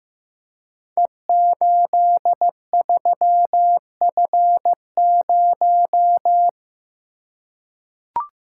Morse code, heard as a keyed tone, about 15 wpm, 700 Hz. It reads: E83F0
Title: Code Groups - 5 Letter:Numbers - 15wpm